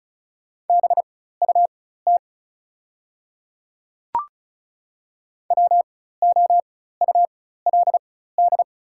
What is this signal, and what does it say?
Morse code, heard as a keyed tone, about 36 wpm, spaced as for 16 wpm, 700 Hz. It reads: BUT WOULD
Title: 4x Character Spacing - Top 100 Words - 35wpm